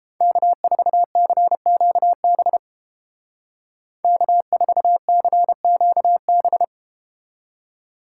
Morse code, heard as a keyed tone, about 32 wpm, 700 Hz. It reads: K4CQB K4CQB